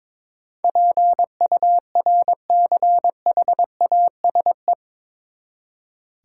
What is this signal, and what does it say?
Morse code, heard as a keyed tone, about 21 wpm, 700 Hz. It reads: PURCHASE
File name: practice / 22wpm / Eight Letter Words - 22wpm